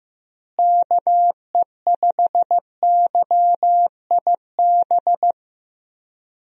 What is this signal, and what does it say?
Morse code, heard as a keyed tone, about 15 wpm, 700 Hz. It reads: KE5YIB
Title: USA Call Signs - 6 Characters - Repeated 3x - 15wpm